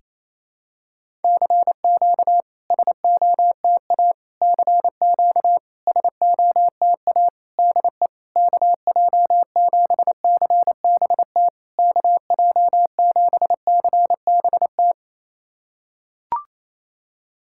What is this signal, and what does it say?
Morse code, heard as a keyed tone, about 28 wpm, 700 Hz. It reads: CQ SOTA CQ SOTA DE KJ7CBT KJ7CBT
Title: SOTA QSO Practice 28wpm